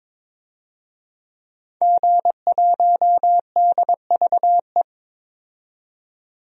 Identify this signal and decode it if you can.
Morse code, heard as a keyed tone, about 22 wpm, 700 Hz. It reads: G1DVE
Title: DXCC-100 Call Signs - 5 Characters - 22wpm